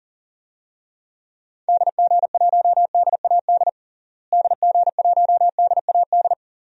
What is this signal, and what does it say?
Morse code, heard as a keyed tone, about 37 wpm, 700 Hz. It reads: DG1DAD DG1DAD
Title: DXCC-100 Call Signs - 6 Characters - Repeated 2x - 40wpm